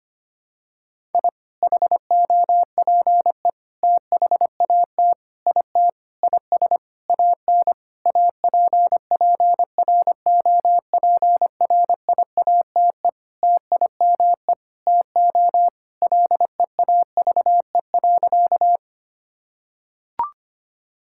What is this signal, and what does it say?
Morse code, heard as a keyed tone, about 25 wpm, 700 Hz. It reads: I HOPE THAT IT IS AN APPROPRIATE TIME TO LEAVE.
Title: Sentences from Top 1000 Words - Review Part 5 - 25wpm